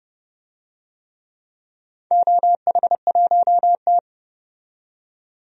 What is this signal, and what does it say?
Morse code, heard as a keyed tone, about 30 wpm, 700 Hz. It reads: OH1T
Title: DXCC-100 Call Signs - 4 Characters - 30wpm